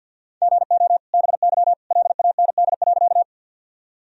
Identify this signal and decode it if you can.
Morse code, heard as a keyed tone, about 52 wpm, 700 Hz. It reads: GO BY LAND.
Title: Sentences from Top 1000 Words - Review Part 1 - 50wpm